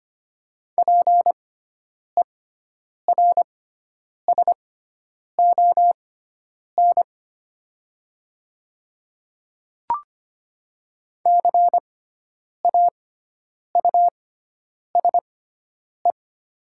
Morse code, heard as a keyed tone, about 25 wpm, spaced as for 9 wpm, 700 Hz. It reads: PERSON CAUSE
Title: 6x Character Spacing - Top 500 Words - 25wpm